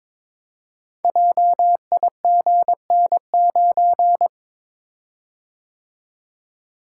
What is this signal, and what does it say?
Morse code, heard as a keyed tone, about 23 wpm, 700 Hz. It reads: JIGN9
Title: Code Groups - 5 Letter:Numbers - 22wpm